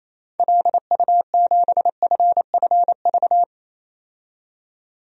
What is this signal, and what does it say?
Morse code, heard as a keyed tone, about 28 wpm, 700 Hz. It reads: LU7FFV
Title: DXCC-100 Call Signs - 6 Characters - 28wpm